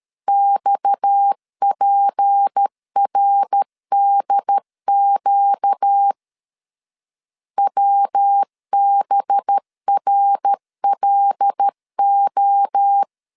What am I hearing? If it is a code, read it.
Morse code, heard as a keyed tone, about 13 wpm, 800 Hz. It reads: XPRDQ WBRLO
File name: LEZIONI / 6 PKX0 / 6 PKX0 12wpm 2231